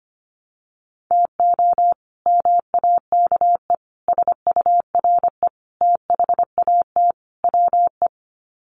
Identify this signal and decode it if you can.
Morse code, heard as a keyed tone, about 25 wpm, 700 Hz. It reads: TO MAKE SURE THAT WE
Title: Sets of 5 Words Encore 25wpm